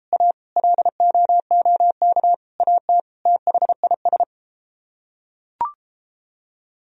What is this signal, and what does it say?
Morse code, heard as a keyed tone, about 31 wpm, 700 Hz. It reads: A LOOK AT THIS